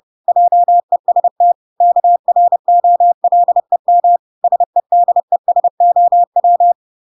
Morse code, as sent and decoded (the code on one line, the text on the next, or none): .--- . ... - / -.- .-. --- .-.. -- / ... -.. ... --- .--
JEST KROLEM SEDESOW